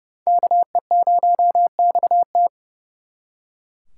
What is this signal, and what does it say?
Morse code, heard as a keyed tone, about 39 wpm, 700 Hz. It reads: KE0XT